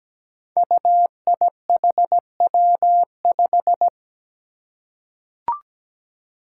Morse code, heard as a keyed tone, about 17 wpm, 700 Hz. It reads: UIHW5
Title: Five Challenging Letter-Number Combinations 17wpm